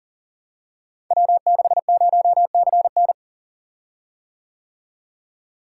Morse code, heard as a keyed tone, about 42 wpm, 700 Hz. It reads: W60CN
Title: Code Groups - 5 Letter:Numbers - 40wpm